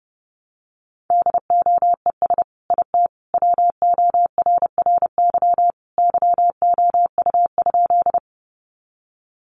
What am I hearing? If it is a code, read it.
Morse code, heard as a keyed tone, about 30 wpm, 700 Hz. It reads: DOES IT WORRY YOU?
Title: Sentences from Top 1100 Words 30wpm